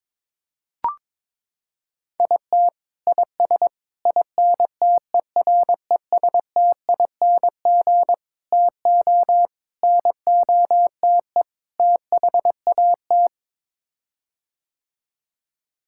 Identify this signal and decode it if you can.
Morse code, heard as a keyed tone, about 22 wpm, 700 Hz. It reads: IT IS INTERESTING TO NOTE THAT
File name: practice / 22wpm / Sets of 6 Words - 22wpm